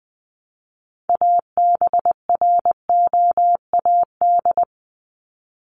Morse code, heard as a keyed tone, about 20 wpm, 700 Hz. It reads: ABROAD
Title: Top 1751-2000 Words 20wpm